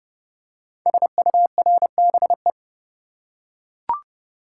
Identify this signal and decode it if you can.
Morse code, heard as a keyed tone, about 30 wpm, 700 Hz. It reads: SURBE
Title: Code Groups - 5 Letters - 30wpm